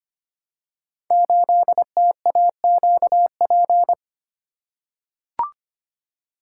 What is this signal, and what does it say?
Morse code, heard as a keyed tone, about 25 wpm, 700 Hz. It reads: 8TAQP
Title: Code Groups - 5 Letter:Numbers - 25wpm